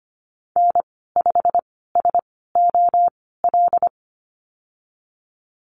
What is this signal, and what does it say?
Morse code, heard as a keyed tone, about 25 wpm, spaced as for 16 wpm, 690 Hz. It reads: N5SOL
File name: Lesson 15g - Number 5 - Call Signs 25wpm